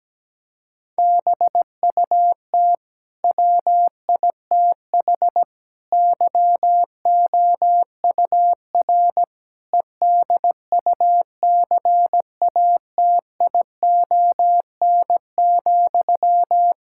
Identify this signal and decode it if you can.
Morse code, heard as a keyed tone, about 17 wpm, 700 Hz. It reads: BUT WITH YOUR EDUCATION,
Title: Sentences from Top 1000 Words - Review Part 6 - 17wpm